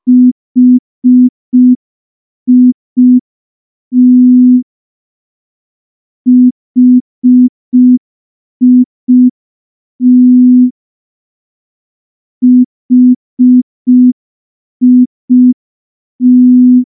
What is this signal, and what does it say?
Morse code, heard as a keyed tone, about 5 wpm, 250 Hz. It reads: HIT HIT HIT
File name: morse / morse-training-05WPM